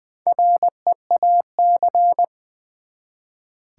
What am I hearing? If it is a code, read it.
Morse code, heard as a keyed tone, about 20 wpm, 700 Hz. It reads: REAC
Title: Build A Word - 1 Letter At A Time - 5-Letter Words - 20wpm